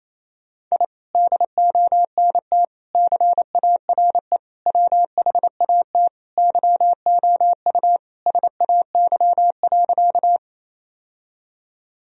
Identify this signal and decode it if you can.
Morse code, heard as a keyed tone, about 28 wpm, 700 Hz. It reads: I DONT CARE WHAT YOU SAY.
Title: Sentences from Top 1000 Words - Review Part 1 - 28wpm